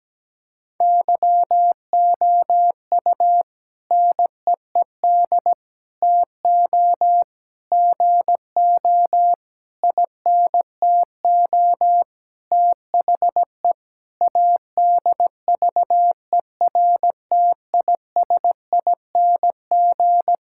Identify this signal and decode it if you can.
Morse code, heard as a keyed tone, about 17 wpm, 700 Hz. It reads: YOU NEED TO GO INTO THE ADVERTISING
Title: Sentences from Top 1000 Words - Review Part 6 - 17wpm